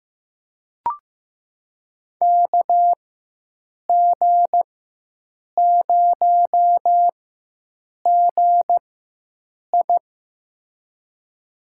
Morse code, heard as a keyed tone, about 15 wpm, spaced as for 7 wpm, 700 Hz. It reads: KG0GI
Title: USA Call Signs - 5 Characters - 4x Character Spacing - 15wpm